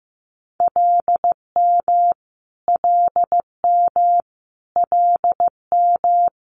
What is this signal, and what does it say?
Morse code, heard as a keyed tone, about 15 wpm, 700 Hz. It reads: LM LM LM
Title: Multi-letter Phonemes Repeated 3x 15wpm